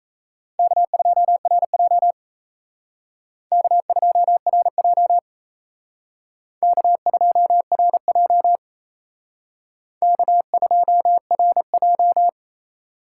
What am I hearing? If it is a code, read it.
Morse code, heard as a keyed tone, about 34 wpm, 700 Hz. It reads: K2RJ K2RJ K2RJ K2RJ
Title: Speed Racing - USA 1x2 Call Signs 28wpm